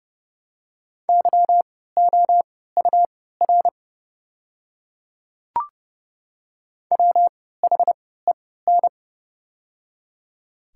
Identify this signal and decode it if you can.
Morse code, heard as a keyed tone, about 30 wpm, spaced as for 17 wpm, 700 Hz. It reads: YOUR WHEN
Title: 3x Character Spacing - Top 100 Words - 30wpm